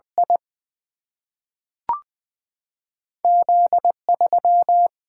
Morse code, heard as a keyed tone, about 20 wpm, 700 Hz. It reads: I Z 3
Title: Single-Letter Single-Number 20wpm